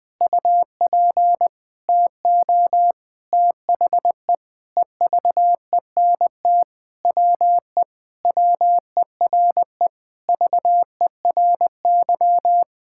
Morse code, heard as a keyed tone, about 20 wpm, 700 Hz. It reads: UP TO THE EVENT WE WERE VERY
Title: Sentences with 7-Word Phrases - Spoken Beginning - 20wpm